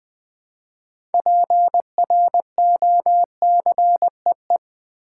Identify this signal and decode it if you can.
Morse code, heard as a keyed tone, about 20 wpm, 700 Hz. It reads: PROCEE
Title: Build A Word - 3 Letters At A Time - 8-Letter Words - 20wpm